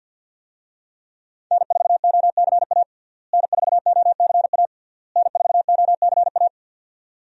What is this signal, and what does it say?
Morse code, heard as a keyed tone, about 49 wpm, 700 Hz. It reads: N4OCA N4OCA N4OCA
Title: USA Call Signs - 5 Characters - Repeated 3x - 50wpm